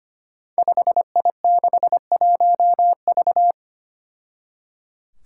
Morse code, heard as a keyed tone, about 25 wpm, 700 Hz. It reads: 5I61V
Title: Five Challenging Letter-Number Combinations 25wpm